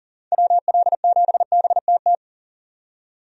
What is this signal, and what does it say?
Morse code, heard as a keyed tone, about 40 wpm, 700 Hz. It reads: WL7BTT
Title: USA Call Signs - 6 Characters - 40wpm